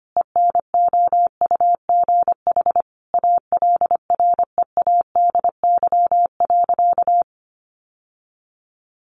Morse code, heard as a keyed tone, about 25 wpm, 700 Hz. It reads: ENOUGH ALREADY.
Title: Sentences from Top 700 Words 25wpm